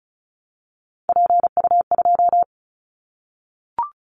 Morse code, heard as a keyed tone, about 36 wpm, 700 Hz. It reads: PU2
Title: Callsign Prefixes for Top 50 DXCC - 35wpm